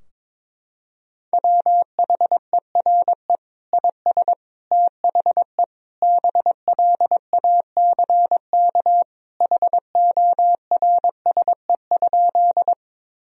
Morse code, heard as a keyed tone, about 22 wpm, 700 Hz. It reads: WHERE IS THE BLACK HORSE?
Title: Sentences from Top 1000 Words - Review Part 2 - 22wpm